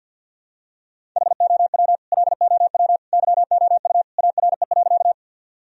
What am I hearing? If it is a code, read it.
Morse code, heard as a keyed tone, about 49 wpm, 700 Hz. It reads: HOW LOW YOU ARE.
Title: Sentences from Top 1000 Words - Review Part 1 - 50wpm